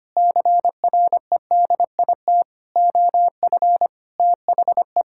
Morse code, heard as a keyed tone, about 25 wpm, 700 Hz. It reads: CREDIT OF THE
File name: Sets of 9 Words - 25wpm